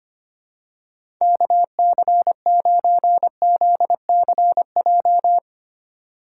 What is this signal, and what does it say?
Morse code, heard as a keyed tone, about 25 wpm, 700 Hz. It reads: KC9ZCJ
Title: USA Call Signs - 6 Characters - 3x Character Spacing - 25wpm